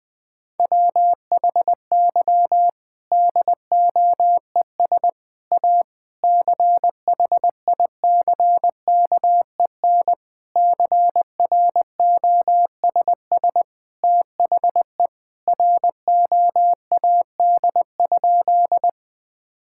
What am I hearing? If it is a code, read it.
Morse code, heard as a keyed tone, about 21 wpm, 700 Hz. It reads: WHY DOES A CHICKEN CROSS THE ROAD?
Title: Sentences from Top 1000 Words - Review Part 5 - 20wpm